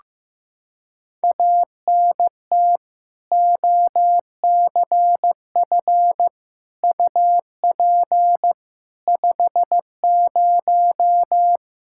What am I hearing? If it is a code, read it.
Morse code, heard as a keyed tone, about 15 wpm, 700 Hz. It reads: ANT OCF UP 50
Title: Practice QSO - Exchange 2 - Info - 15wpm